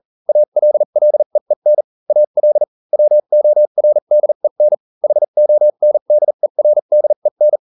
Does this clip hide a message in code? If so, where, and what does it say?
Yes: Morse code from 0.3 to 7.6 s, heard as a keyed tone, about 40 wpm, spaced as for 35 wpm, 600 Hz. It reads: ALLEEN AL WORDEN HONDERDEN